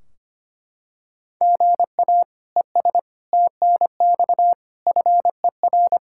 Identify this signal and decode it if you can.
Morse code, heard as a keyed tone, about 25 wpm, 700 Hz. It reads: GA ES TNX FER
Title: Practice QSO - Exchange 1 - Beginning - 25wpm